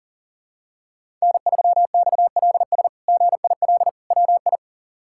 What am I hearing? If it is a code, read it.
Morse code, heard as a keyed tone, about 40 wpm, 700 Hz. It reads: N3XLS GIL WI
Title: K1USN SST QSO Practice - 40wpm